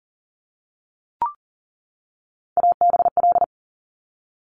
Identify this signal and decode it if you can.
Morse code, heard as a keyed tone, about 40 wpm, 700 Hz. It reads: ABL
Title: Top Three Letter Combinations in Words 40wpm